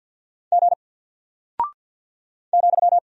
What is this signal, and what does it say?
Morse code, heard as a keyed tone, about 50 wpm, 700 Hz. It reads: G ,
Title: Single Character (Letter:Number:Punctuation:Prosign) Alt 50wpm